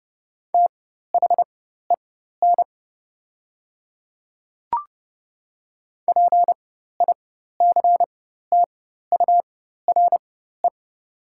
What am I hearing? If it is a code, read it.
Morse code, heard as a keyed tone, about 30 wpm, spaced as for 14 wpm, 700 Hz. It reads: THEN PICTURE
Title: 4x Character Spacing - Top 200 Words - 30wpm